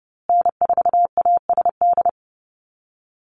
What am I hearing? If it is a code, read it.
Morse code, heard as a keyed tone, about 30 wpm, 700 Hz. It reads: N4ASD
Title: USA Call Signs Repeated 2x 30wpm